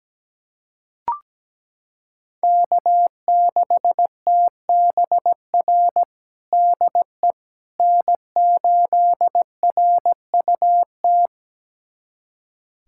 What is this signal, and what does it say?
Morse code, heard as a keyed tone, about 17 wpm, 700 Hz. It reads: K6TBR DE N8RUT
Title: USA Callsign DE Callsign - 5 Characters - 17wpm